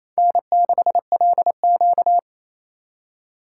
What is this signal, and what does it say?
Morse code, heard as a keyed tone, about 28 wpm, 700 Hz. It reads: N6LQ
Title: USA Call Signs - 4 Characters - 2x Character Spacing - 28wpm